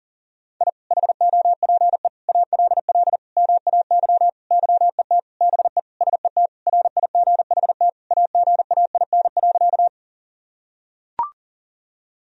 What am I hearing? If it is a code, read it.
Morse code, heard as a keyed tone, about 40 wpm, 700 Hz. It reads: I HOPE ALL MAY YET BE SET RIGHT AGAIN.
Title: Sentences from Top 1000 Words - Review Part 1 - 40wpm